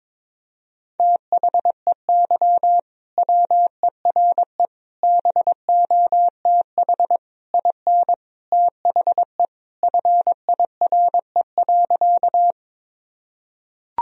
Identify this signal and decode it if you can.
Morse code, heard as a keyed tone, about 22 wpm, 700 Hz. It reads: THEY WERE BOTH IN THE FIRE.
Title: Sentences from Top 1000 Words - Review Part 1 - 22wpm